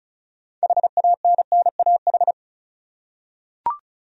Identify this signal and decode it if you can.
Morse code, heard as a keyed tone, about 36 wpm, 700 Hz. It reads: HANNAH
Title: USA Common Names - 35wpm